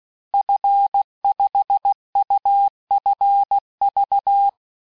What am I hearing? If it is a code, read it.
Morse code, heard as a keyed tone, about 16 wpm, 800 Hz. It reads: F5UFV
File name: cours-du-lundi files / cours-1580-fin-QSO